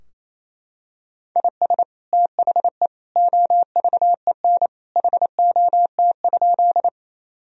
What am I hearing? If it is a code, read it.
Morse code, heard as a keyed tone, about 28 wpm, 700 Hz. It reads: IS THE OVEN HOT?
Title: Sentences from Top 1000 Words - Review Part 4 - 28wpm